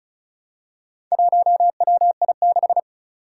Morse code, heard as a keyed tone, about 36 wpm, 700 Hz. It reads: 1WI6
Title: Four Challenging Letter-Number Combinations 35wpm